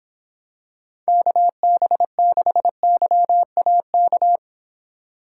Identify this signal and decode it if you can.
Morse code, heard as a keyed tone, about 26 wpm, 700 Hz. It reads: KB6YAK